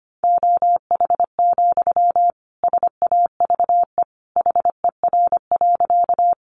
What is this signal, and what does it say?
Morse code, heard as a keyed tone, about 25 wpm, 700 Hz. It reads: OH, SAVE HER.
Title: Sentences from Top 1100 Words 25wpm